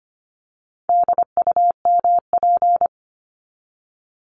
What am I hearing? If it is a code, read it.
Morse code, heard as a keyed tone, about 25 wpm, 700 Hz. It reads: DUMP
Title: Top 1001-1250 Words 25wpm